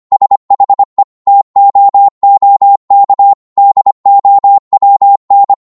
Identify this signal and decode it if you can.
Morse code, heard as a keyed tone, about 25 wpm, 800 Hz. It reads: SHE TOOK DOWN